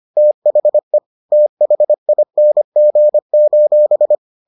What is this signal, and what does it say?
Morse code, heard as a keyed tone, about 25 wpm, 600 Hz. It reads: THE THING: